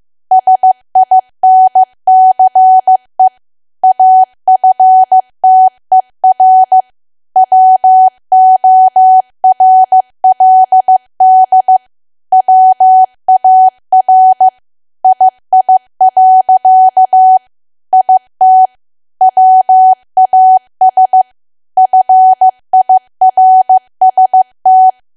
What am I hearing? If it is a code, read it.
Morse code, heard as a keyed tone, about 15 wpm, 750 Hz. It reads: SINCE AFTER WORLD WAR II. IT WAS FIRST
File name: cwa / W1AW-Level2 / 15WPM